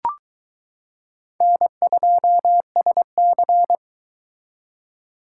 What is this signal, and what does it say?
Morse code, heard as a keyed tone, about 21 wpm, 700 Hz. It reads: N2SC